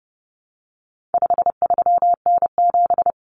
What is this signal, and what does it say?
Morse code, heard as a keyed tone, about 30 wpm, 700 Hz. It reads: 53N7
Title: Common Cut Numbers - 4-Digit 30wpm